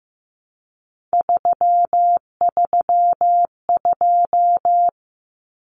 Morse code, heard as a keyed tone, about 15 wpm, 700 Hz. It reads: 332
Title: Common Cut Numbers - 3-Digit 15wpm